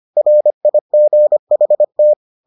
Morse code, heard as a keyed tone, about 25 wpm, 600 Hz. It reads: RIGHT